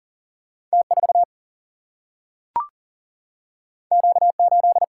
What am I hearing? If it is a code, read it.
Morse code, heard as a keyed tone, about 39 wpm, 700 Hz. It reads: T 4 Q 8
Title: Single-Letter Single-Number 40wpm